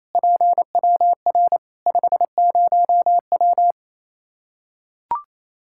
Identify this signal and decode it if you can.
Morse code, heard as a keyed tone, about 28 wpm, 700 Hz. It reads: PWR 50W
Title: K1USN SST QSO Practice 28wpm